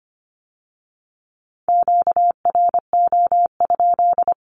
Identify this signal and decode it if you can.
Morse code, heard as a keyed tone, about 25 wpm, 700 Hz. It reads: QRO?
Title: Q-Codes with Spoken Meaning - 25wpm